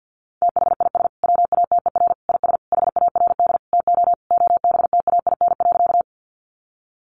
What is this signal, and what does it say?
Morse code, heard as a keyed tone, about 50 wpm, 700 Hz. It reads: THIS WATER IS HARD TO OBTAIN.
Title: Sentences from Top 1300 Words 50wpm